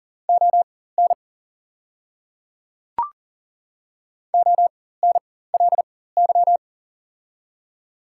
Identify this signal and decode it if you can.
Morse code, heard as a keyed tone, about 43 wpm, spaced as for 19 wpm, 690 Hz. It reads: ON ONLY
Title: 4x Character Spacing - Top 300 Words - 40wpm